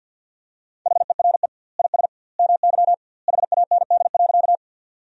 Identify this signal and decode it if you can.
Morse code, heard as a keyed tone, about 50 wpm, 700 Hz. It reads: HERE IS MY HAND.
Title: 2x Word-Spacing - Sentences from Top 200 Words - 50wpm